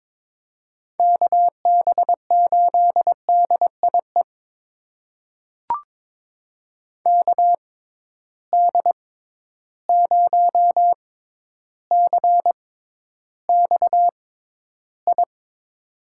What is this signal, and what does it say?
Morse code, heard as a keyed tone, about 22 wpm, spaced as for 11 wpm, 700 Hz. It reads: KB8DIE KD0CXI
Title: USA Call Signs - 6 Characters - 6x Character Spacing - 22wpm